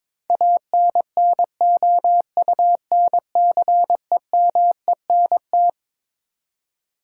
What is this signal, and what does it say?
Morse code, heard as a keyed tone, about 22 wpm, 700 Hz. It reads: ANNOUNCEMENT
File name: Twelve Letter Words - 22wpm